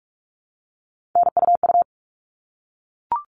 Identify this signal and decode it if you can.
Morse code, heard as a keyed tone, about 44 wpm, 700 Hz. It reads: NUU
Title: All Cut Numbers - 3-Digit 45wpm